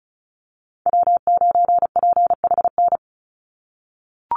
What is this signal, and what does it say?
Morse code, heard as a keyed tone, about 34 wpm, 700 Hz. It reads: W9PHN
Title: USA Call Signs 35wpm